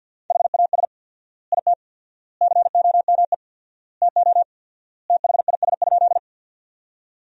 Morse code, heard as a keyed tone, about 48 wpm, 700 Hz. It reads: HAS IT COME TO THIS?
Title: 4x Word-Spacing - Sentences from Top 100 Words - 50wpm